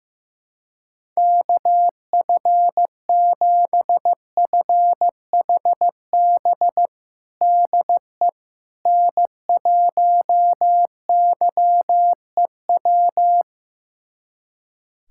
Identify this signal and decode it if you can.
Morse code, heard as a keyed tone, about 15 wpm, 700 Hz. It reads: KF7FHB DE N1YEW
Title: USA Callsign DE Callsign 15wpm